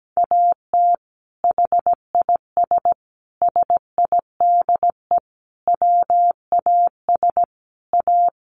Morse code, heard as a keyed tone, about 17 wpm, 700 Hz. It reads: AT HIS SIDE WAS A
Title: Sentences from Top 500 Words 17wpm